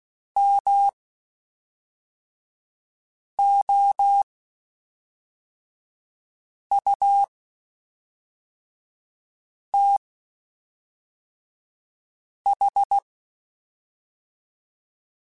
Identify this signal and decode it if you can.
Morse code, heard as a keyed tone, about 16 wpm, spaced as for 3 wpm, 800 Hz. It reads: MOUTH
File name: Spaced / Chuckles Spaced-mp3s / BrainMouth sp @5